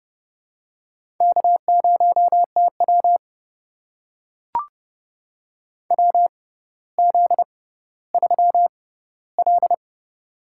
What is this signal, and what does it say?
Morse code, heard as a keyed tone, about 31 wpm, spaced as for 16 wpm, 700 Hz. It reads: K0TW WZ3L
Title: USA Call Signs - 4 Characters - 6x Character Spacing - 30wpm